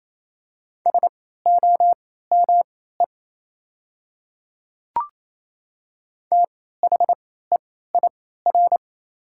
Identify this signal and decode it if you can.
Morse code, heard as a keyed tone, about 28 wpm, spaced as for 16 wpm, 690 Hz. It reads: SOME THEIR
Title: 3x Character Spacing - Top 100 Words - 28wpm